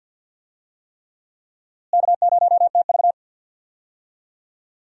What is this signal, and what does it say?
Morse code, heard as a keyed tone, about 52 wpm, 700 Hz. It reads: K0TV
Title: USA Call Signs - 4 Characters - Repeated 3x - 50wpm